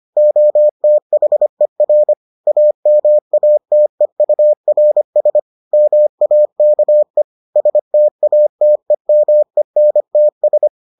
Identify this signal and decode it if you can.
Morse code, heard as a keyed tone, about 25 wpm, 600 Hz. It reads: OTHER AMATEURS MAKE STATEMENTS